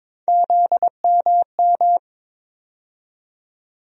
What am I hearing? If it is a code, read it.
Morse code, heard as a keyed tone, about 22 wpm, 700 Hz. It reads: ZMM
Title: USA Call Sign Postfixes - 3 Letters 22wpm